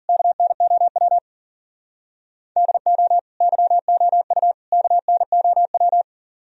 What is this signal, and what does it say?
Morse code, heard as a keyed tone, about 50 wpm, 700 Hz. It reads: KNOW DO YOU KNOW
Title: Speed-Racing - Sets of 3 Words 40wpm